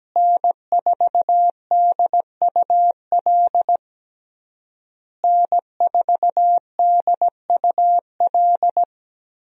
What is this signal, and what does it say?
Morse code, heard as a keyed tone, about 17 wpm, 700 Hz. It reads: N4DUL N4DUL
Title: USA Call Signs - 5 Characters - Repeated 3x - 17wpm